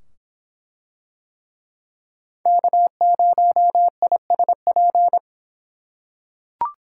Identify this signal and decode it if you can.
Morse code, heard as a keyed tone, about 25 wpm, 700 Hz. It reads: K0ISP